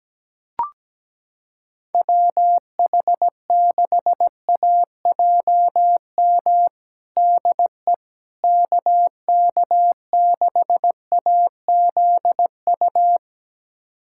Morse code, heard as a keyed tone, about 17 wpm, 700 Hz. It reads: WH6AJM DE KK6AZU
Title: USA Callsign DE Callsign 17wpm